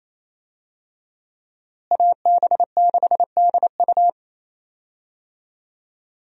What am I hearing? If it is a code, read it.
Morse code, heard as a keyed tone, about 27 wpm, 700 Hz. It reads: AB6DU
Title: USA Call Signs - 5 Characters - Repeated 2x - 28wpm